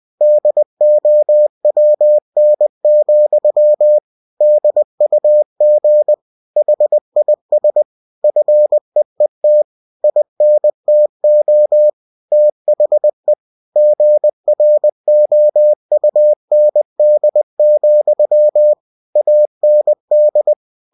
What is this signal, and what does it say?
Morse code, heard as a keyed tone, about 20 wpm, 600 Hz. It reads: DOWN, DUG HIS FEET INTO THE GROUND, AND
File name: cw-books / winnie-the-pooh / winnie-20wpm-page-0076